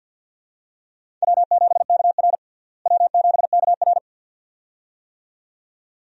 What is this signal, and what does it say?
Morse code, heard as a keyed tone, about 52 wpm, 700 Hz. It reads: W7KR W7KR
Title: USA Call Signs - 4 Characters - Repeated 2x - 50wpm